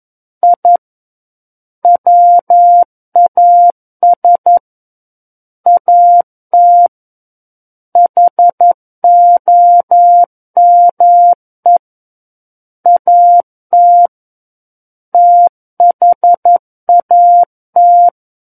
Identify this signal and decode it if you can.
Morse code, heard as a keyed tone, about 11 wpm, 700 Hz. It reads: I WAS AT HOME AT THAT
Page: War of the Worlds - 02-Chapter 2 - 11 WPM